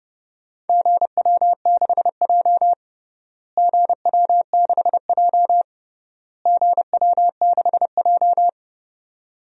Morse code, heard as a keyed tone, about 30 wpm, 700 Hz. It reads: GW6J GW6J GW6J
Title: DXCC-100 Call Signs - 4 Characters - Repeated 3x - 30wpm